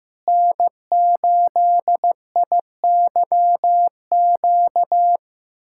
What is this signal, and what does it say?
Morse code, heard as a keyed tone, about 15 wpm, 700 Hz. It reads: N8IYQ
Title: USA Callsign DE Callsign 15wpm